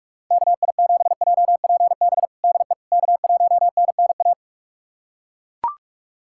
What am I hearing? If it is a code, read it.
Morse code, heard as a keyed tone, about 41 wpm, 700 Hz. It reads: KI7JPB DE K1NNA
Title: USA Callsign DE Callsign 45wpm